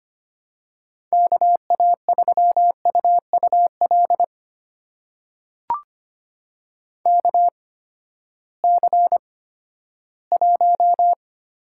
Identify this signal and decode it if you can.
Morse code, heard as a keyed tone, about 25 wpm, spaced as for 14 wpm, 700 Hz. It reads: KA3UUL KC1
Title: USA Call Signs - 8x Character Spacing - 25wpm